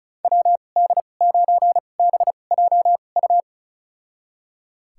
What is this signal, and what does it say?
Morse code, heard as a keyed tone, about 34 wpm, spaced as for 25 wpm, 700 Hz. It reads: WD9BJU
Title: USA Call Signs - 6 Characters - 2x Character Spacing - 35wpm